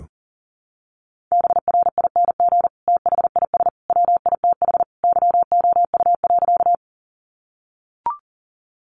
Morse code, heard as a keyed tone, about 40 wpm, 700 Hz. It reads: BRING THIS WITH YOU.
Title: Sentences from Top 500 Words 40wpm